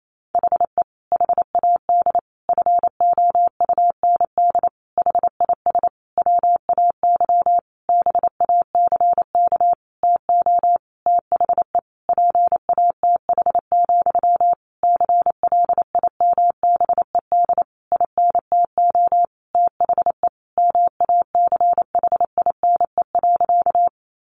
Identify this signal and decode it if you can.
Morse code, heard as a keyed tone, about 28 wpm, 700 Hz. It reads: HE HAD FOUND HIS WAY BACK TO THE PATH, CLIMBED INTO THE MACHINE.
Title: A Sound Of Thunder 28wpm